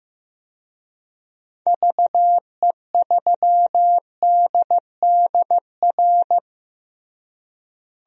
Morse code, heard as a keyed tone, about 15 wpm, 700 Hz. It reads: VE3DDR
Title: DXCC-100 Call Signs - 6 Characters - Repeated 3x - 15wpm